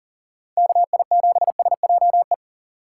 Morse code, heard as a keyed tone, about 40 wpm, 700 Hz. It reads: KI7SJE
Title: USA Call Signs - 3x Character Spacing - 40wpm